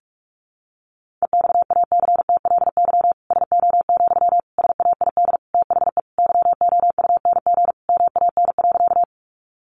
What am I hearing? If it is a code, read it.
Morse code, heard as a keyed tone, about 44 wpm, 700 Hz. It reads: EXACTLY SO, SAID THE YOUNG MAN.
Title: Sentences from Top 900 Words 45wpm